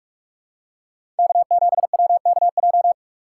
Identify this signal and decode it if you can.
Morse code, heard as a keyed tone, about 44 wpm, 690 Hz. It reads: K7WKJ
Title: USA Call Signs - 5 Characters - 6x Character Spacing - 45wpm